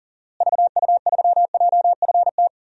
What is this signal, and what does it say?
Morse code, heard as a keyed tone, about 40 wpm, 700 Hz. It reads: VU3JFT
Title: DXCC-100 Call Signs - 6 Characters - Repeated 3x - 40wpm